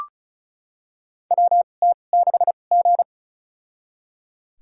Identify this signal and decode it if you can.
Morse code, heard as a keyed tone, about 36 wpm, spaced as for 25 wpm, 700 Hz. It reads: WT6G
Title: USA Call Signs - 2x Character Spacing - 35wpm